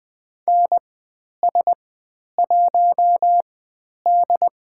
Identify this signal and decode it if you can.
Morse code, heard as a keyed tone, about 20 wpm, spaced as for 10 wpm, 700 Hz. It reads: NS1D
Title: 102 Lesson 17g - Letter u - Call Signs 20wpm